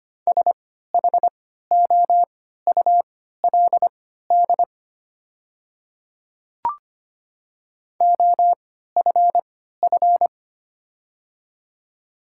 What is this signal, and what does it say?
Morse code, heard as a keyed tone, about 25 wpm, spaced as for 14 wpm, 700 Hz. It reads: SHOULD OFF
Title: 3x Character Spacing - Top 500 Words - 25wpm